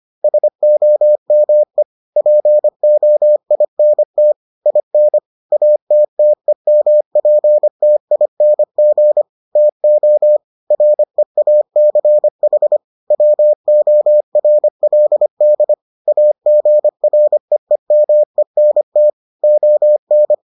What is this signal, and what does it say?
Morse code, heard as a keyed tone, about 25 wpm, 600 Hz. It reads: SOME POINT IN ATTEMPTING TO REACH WORLD AGREEMENT ON